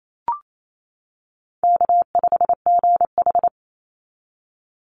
Morse code, heard as a keyed tone, about 28 wpm, 700 Hz. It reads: K5GH
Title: Easy Call Signs - 28wpm